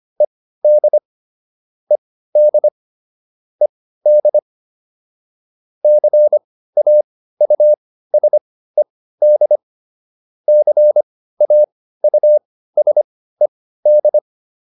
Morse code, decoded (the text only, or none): ED ED ED CAUSED CAUSED